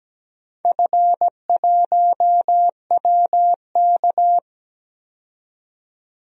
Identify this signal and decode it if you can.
Morse code, heard as a keyed tone, about 17 wpm, 700 Hz. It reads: F1WK
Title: DXCC-100 Call Signs - 4 Characters - 17wpm